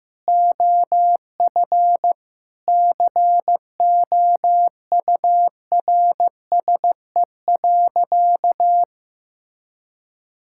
Morse code, heard as a keyed tone, about 16 wpm, 700 Hz. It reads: OF COURSE.
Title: Sentences from Top 1000 Words - Review Part 5 - 15wpm